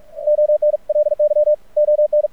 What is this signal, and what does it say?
Morse code, heard as a keyed tone, about 46 wpm, 600 Hz. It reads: ONLY ON
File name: archived / CRSnet